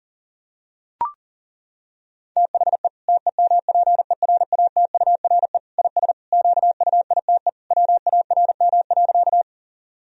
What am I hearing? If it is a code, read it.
Morse code, heard as a keyed tone, about 40 wpm, 700 Hz. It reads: THE TEMPERATURE IS QUITE WARM.
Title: Sentences from Top 1000 Words - Review Part 4 - 40wpm